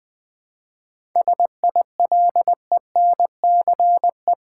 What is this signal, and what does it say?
Morse code, heard as a keyed tone, about 20 wpm, 700 Hz. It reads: SILENCE
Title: 7 Letter Words - First 3 Letters Given - 20wpm